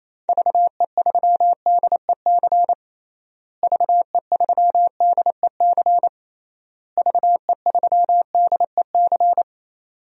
Morse code, heard as a keyed tone, about 28 wpm, 700 Hz. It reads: VE3DEC VE3DEC VE3DEC
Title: DXCC-100 Call Signs - 6 Characters - Repeated 3x - 28wpm